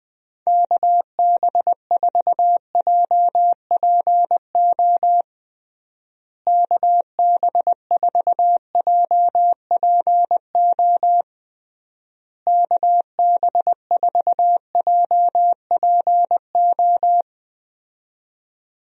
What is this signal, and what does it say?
Morse code, heard as a keyed tone, about 20 wpm, 700 Hz. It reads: KB4JPO KB4JPO KB4JPO
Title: USA Call Signs - 6 Characters - Repeated 3x - 20wpm